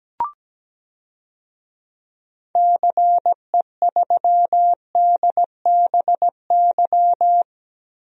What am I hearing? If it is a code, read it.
Morse code, heard as a keyed tone, about 17 wpm, 700 Hz. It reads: CE3DBY
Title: DXCC-100 Call Signs - 6 Characters - 17wpm